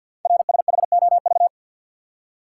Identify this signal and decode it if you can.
Morse code, heard as a keyed tone, about 48 wpm, 700 Hz. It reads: RSHOV
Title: Code Groups - 5 Letter:Numbers - 50wpm